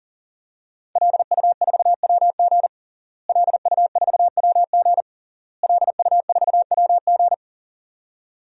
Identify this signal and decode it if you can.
Morse code, heard as a keyed tone, about 40 wpm, 700 Hz. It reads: LU4WG LU4WG LU4WG
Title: DXCC-100 Call Signs - 5 Characters - Repeated 3x - 40wpm